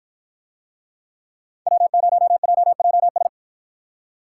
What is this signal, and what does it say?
Morse code, heard as a keyed tone, about 53 wpm, 700 Hz. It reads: W0JJS